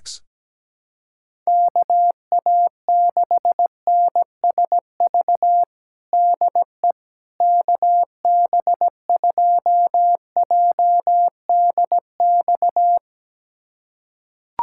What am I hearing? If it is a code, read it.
Morse code, heard as a keyed tone, about 17 wpm, 700 Hz. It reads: KA6NSV DE KB2JDX
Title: USA Callsign DE Callsign 17wpm